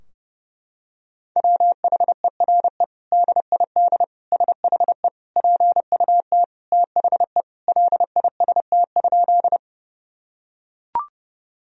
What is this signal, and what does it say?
Morse code, heard as a keyed tone, about 30 wpm, 700 Hz. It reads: WHERE DID SHE PUT THE LIST?
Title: Sentences from Top 1000 Words - Review Part 2 - 30wpm